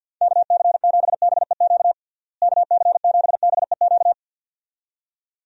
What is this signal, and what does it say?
Morse code, heard as a keyed tone, about 50 wpm, 700 Hz. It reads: KC7BEQ KC7BEQ
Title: USA Call Signs - 6 Characters - Repeated 2x - 50wpm